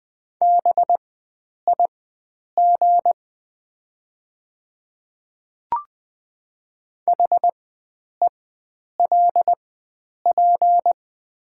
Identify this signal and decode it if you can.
Morse code, heard as a keyed tone, about 20 wpm, spaced as for 9 wpm, 700 Hz. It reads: BIG HELP
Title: 4x Character Spacing - Top 300 Words - 20wpm